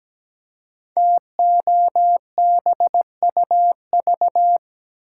Morse code, heard as a keyed tone, about 17 wpm, 700 Hz. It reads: TOBUV
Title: Code Groups - 5 Letters - 17wpm